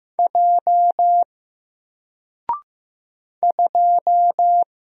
Morse code, heard as a keyed tone, about 15 wpm, 700 Hz. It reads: J 2
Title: Single Letter-Number Alt 15wpm